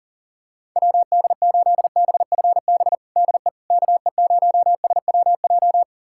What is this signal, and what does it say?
Morse code, heard as a keyed tone, about 40 wpm, 700 Hz. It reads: WD8BFB DE KE0SWJ
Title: USA Callsign DE Callsign - 6 Characters - 40wpm